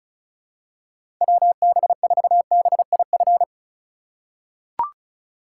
Morse code, heard as a keyed tone, about 34 wpm, 690 Hz. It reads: WB4BIF
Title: USA Call Signs - 6 Characters - 3x Character Spacing - 35wpm